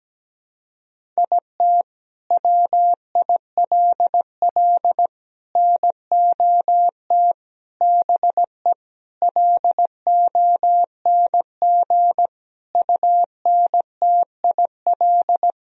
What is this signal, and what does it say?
Morse code, heard as a keyed tone, about 17 wpm, 700 Hz. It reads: IT WILL NOT BE LONG UNTIL
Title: Sentences from Top 1000 Words - Review Part 1 - 17wpm